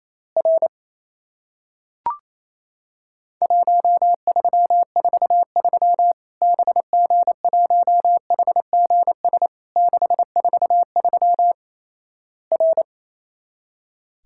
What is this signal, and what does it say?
Morse code, heard as a keyed tone, about 28 wpm, 700 Hz. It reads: R 1343 BG1HGS 643 R
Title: Worked All Europe DX Contest - QTC Exchange - 28wpm